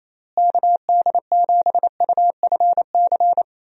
Morse code, heard as a keyed tone, about 28 wpm, 700 Hz. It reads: KD7UFC
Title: USA Call Signs - 6 Characters - Repeated 3x - 28wpm